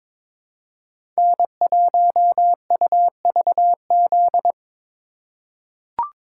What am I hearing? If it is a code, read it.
Morse code, heard as a keyed tone, about 22 wpm, 700 Hz. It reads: N1UVZ
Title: USA Call Signs - 2x Character Spacing - 22wpm